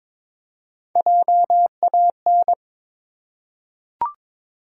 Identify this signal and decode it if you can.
Morse code, heard as a keyed tone, about 22 wpm, 700 Hz. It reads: JAN
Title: USA Common Names - Max 3 Letters - 22wpm